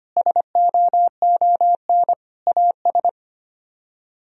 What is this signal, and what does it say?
Morse code, heard as a keyed tone, about 33 wpm, 700 Hz. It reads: SOON AS